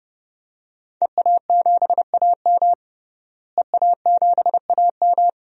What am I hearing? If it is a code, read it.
Morse code, heard as a keyed tone, about 30 wpm, 700 Hz. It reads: EA7AM EA7AM
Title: DXCC-100 Call Signs - 5 Characters - Repeated 2x - 30wpm